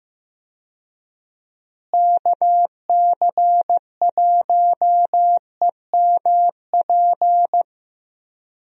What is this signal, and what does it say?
Morse code, heard as a keyed tone, about 15 wpm, 700 Hz. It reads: KC1EMP
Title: USA Call Signs - 6 Characters - Repeated 2x - 15wpm